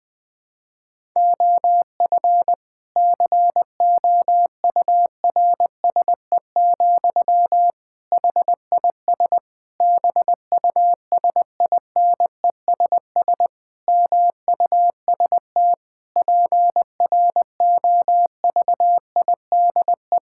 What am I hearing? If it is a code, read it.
Morse code, heard as a keyed tone, about 20 wpm, 700 Hz. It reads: OF COURSE, HIS BUSINESS MUST PROVIDE
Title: Sentences from Top 1000 Words - Review Part 5 - 20wpm